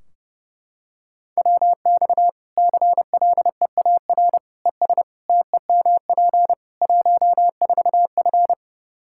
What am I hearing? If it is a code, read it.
Morse code, heard as a keyed tone, about 32 wpm, 700 Hz. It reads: WX CLEAR ES TEMP 14F
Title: K1USN SST QSO Practice 30wpm